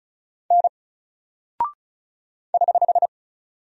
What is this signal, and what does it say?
Morse code, heard as a keyed tone, about 36 wpm, 700 Hz. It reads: N <HH>
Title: Single Character (Letter:Number:Punctuation:Prosign) 35wpm